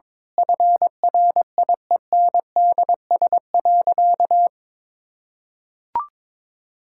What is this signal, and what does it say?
Morse code, heard as a keyed tone, about 22 wpm, 700 Hz. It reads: FRIENDS.
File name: Difficult Sentences with Hints - 22wpm